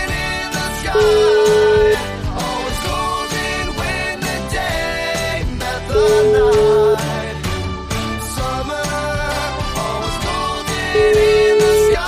Ringback tone (Europe, 425 Hz), with noise.